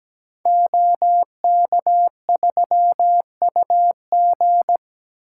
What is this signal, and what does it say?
Morse code, heard as a keyed tone, about 17 wpm, 700 Hz. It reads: OK3UG
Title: Code Groups - 5 Letter:Numbers - 17wpm